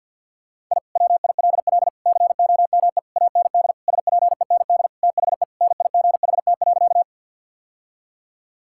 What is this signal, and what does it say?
Morse code, heard as a keyed tone, about 52 wpm, 690 Hz. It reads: I WILL COME AND SPEND THE NIGHT.
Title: Sentences from Top 1000 Words - Review Part 5 - 50wpm